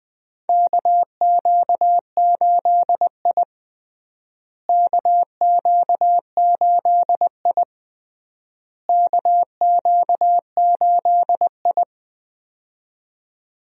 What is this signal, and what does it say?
Morse code, heard as a keyed tone, about 21 wpm, 700 Hz. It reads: KQ8I KQ8I KQ8I
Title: USA Call Signs - 4 Characters - Repeated 3x - 20wpm